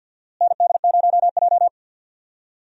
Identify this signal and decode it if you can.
Morse code, heard as a keyed tone, about 53 wpm, 700 Hz. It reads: ND0J
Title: USA Call Signs - 4 Characters - Repeated 2x - 50wpm